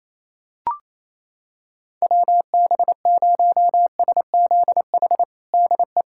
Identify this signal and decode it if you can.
Morse code, heard as a keyed tone, about 27 wpm, 700 Hz. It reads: WB0SZH DE
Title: USA Callsign DE Callsign 28wpm